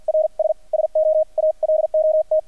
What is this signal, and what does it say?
Morse code, heard as a keyed tone, about 48 wpm, spaced as for 42 wpm, 620 Hz. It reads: RS SO ALOT
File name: archived / CRSnet